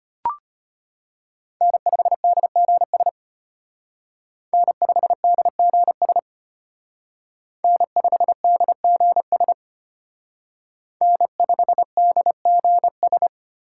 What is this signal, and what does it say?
Morse code, heard as a keyed tone, about 33 wpm, 700 Hz. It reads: N5DGS N5DGS N5DGS N5DGS